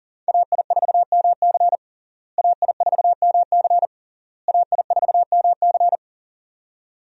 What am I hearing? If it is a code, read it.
Morse code, heard as a keyed tone, about 40 wpm, 700 Hz. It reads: AI4MC AI4MC AI4MC
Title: USA Call Signs - 5 Characters - Repeated 3x - 40wpm